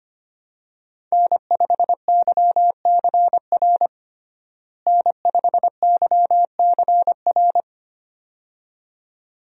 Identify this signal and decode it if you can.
Morse code, heard as a keyed tone, about 25 wpm, 700 Hz. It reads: N5YCR N5YCR
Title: USA Call Signs - 5 Characters - Repeated 2x - 25wpm